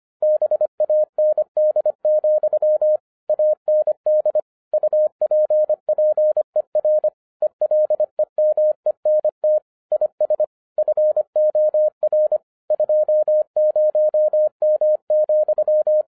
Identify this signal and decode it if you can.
Morse code, heard as a keyed tone, about 25 wpm, 610 Hz. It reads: BAND, AND UPPER ELEMENT IS FOR 20M,